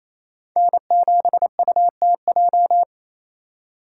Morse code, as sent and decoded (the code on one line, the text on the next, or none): -. --... ..- - .---
N7UTJ